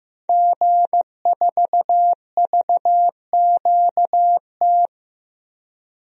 Morse code, heard as a keyed tone, about 15 wpm, 700 Hz. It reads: G4VQT
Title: DXCC-100 Call Signs - 5 Characters - 15wpm